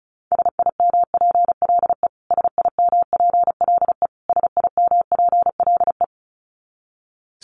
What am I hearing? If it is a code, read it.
Morse code, heard as a keyed tone, about 34 wpm, 700 Hz. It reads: SIMPLE SIMPLE SIMPLE
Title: Top 500 Words Repeated 35wpm